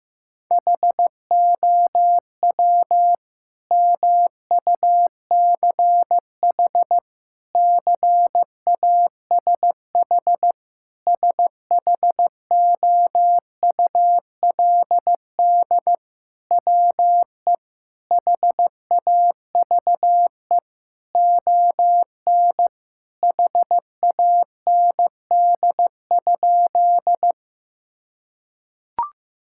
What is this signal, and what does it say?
Morse code, heard as a keyed tone, about 15 wpm, 700 Hz. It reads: HOW MUCH CASH SHOULD WE HAVE ON HAND?
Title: Sentences from Top 1000 Words - Review Part 5 - 15wpm